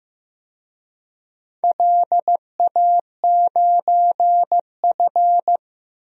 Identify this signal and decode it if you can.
Morse code, heard as a keyed tone, about 15 wpm, 700 Hz. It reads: LA9F
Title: DXCC-100 Call Signs - 4 Characters - 15wpm